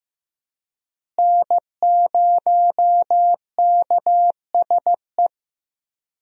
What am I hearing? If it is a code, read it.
Morse code, heard as a keyed tone, about 16 wpm, 700 Hz. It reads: N0KSE
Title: USA Call Signs - 5 Characters - Repeated 2x - 15wpm